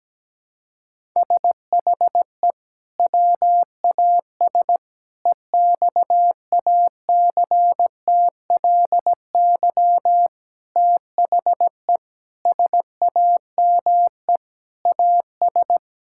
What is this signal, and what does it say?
Morse code, heard as a keyed tone, about 17 wpm, 700 Hz. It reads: SHE WAS EXACTLY THE SAME AS
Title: Sentences from Top 1000 Words - Review Part 6 - 17wpm